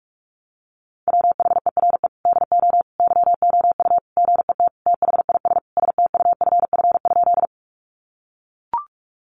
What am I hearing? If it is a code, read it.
Morse code, heard as a keyed tone, about 44 wpm, 700 Hz. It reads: WHERE DO YOU GET THIS STUFF?
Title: Sentences from Top 1400 Words 45wpm